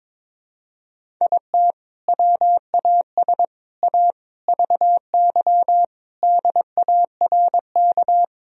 Morse code, heard as a keyed tone, about 22 wpm, 700 Hz. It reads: IT WAS A VY DARK
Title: Sentences from Top 500 Words and Abbreviations 22wpm